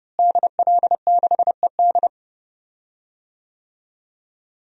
Morse code, heard as a keyed tone, about 29 wpm, 700 Hz. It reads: DL6ED
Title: DXCC-100 Call Signs - 5 Characters - Repeated 2x - 30wpm